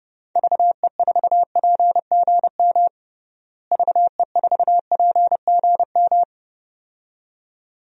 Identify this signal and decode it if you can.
Morse code, heard as a keyed tone, about 30 wpm, 700 Hz. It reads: VE4PGM VE4PGM
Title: DXCC-100 Call Signs - 6 Characters - Repeated 2x - 30wpm